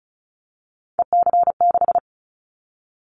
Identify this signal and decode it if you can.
Morse code, heard as a keyed tone, about 36 wpm, 700 Hz. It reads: EC6
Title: Callsign Prefixes for Top 50 DXCC - 35wpm